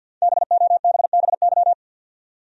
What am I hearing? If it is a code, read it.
Morse code, heard as a keyed tone, about 48 wpm, 700 Hz. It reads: BOBBY
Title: USA Common Names - Max 5 Letters - 50wpm